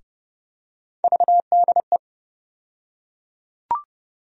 Morse code, heard as a keyed tone, about 30 wpm, 700 Hz. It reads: VDE
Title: USA Call Sign Postfixes - 3 Letters 30wpm